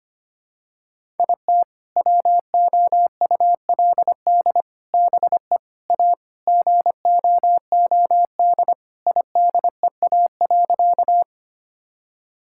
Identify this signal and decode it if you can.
Morse code, heard as a keyed tone, about 25 wpm, 700 Hz. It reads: IT WOULD BE A GOOD IDEA.
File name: practice / 25wpm / Sentences with 3-Word Phrases - 25wpm